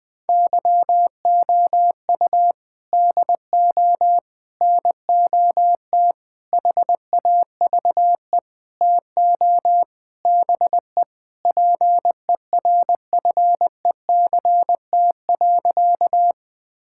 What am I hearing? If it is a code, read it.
Morse code, heard as a keyed tone, about 20 wpm, 700 Hz. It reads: YOU DO NOT HAVE TO BE PERFECT.
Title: Sentences with 5-Word Phrases - Spoken Beginning - 20wpm